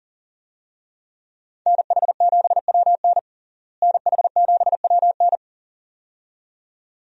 Morse code, heard as a keyed tone, about 40 wpm, 700 Hz. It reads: NH7WN NH7WN
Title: USA Call Signs - 5 Characters - Repeated 2x - 40wpm